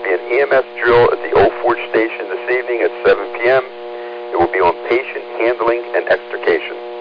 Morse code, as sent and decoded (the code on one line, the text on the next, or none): none